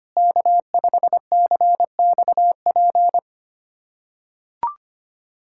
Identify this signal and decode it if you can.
Morse code, heard as a keyed tone, about 25 wpm, 700 Hz. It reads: K5CXP
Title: USA Callsign DE Callsign - 5 Characters - 25wpm